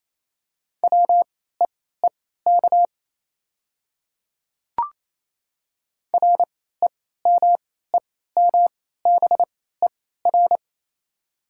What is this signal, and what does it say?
Morse code, heard as a keyed tone, about 28 wpm, spaced as for 16 wpm, 690 Hz. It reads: WEEK REMEMBER
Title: 3x Character Spacing - Top 500 Words - 28wpm